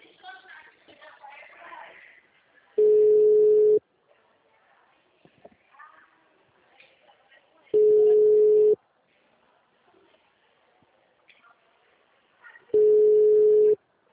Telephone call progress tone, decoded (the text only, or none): ringback tone